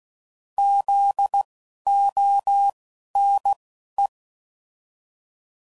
Morse code, heard as a keyed tone, about 16 wpm, spaced as for 11 wpm, 800 Hz. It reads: ZONE
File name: Prefixes / By First Letter / Zs